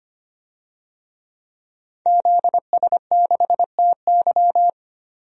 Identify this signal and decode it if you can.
Morse code, heard as a keyed tone, about 25 wpm, 700 Hz. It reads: ZS6TY
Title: DXCC-100 Call Signs - 5 Characters - 25wpm